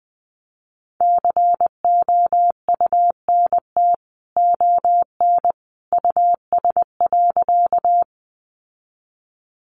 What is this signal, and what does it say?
Morse code, heard as a keyed tone, about 20 wpm, 700 Hz. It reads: COUNT ON US.
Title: Sentences from Top 1500 Words 20wpm